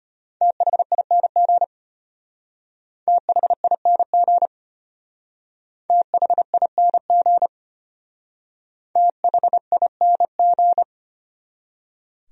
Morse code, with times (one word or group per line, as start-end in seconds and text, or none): 0.4-1.6 THING
3.1-4.5 THING
5.9-7.5 THING
9.0-10.8 THING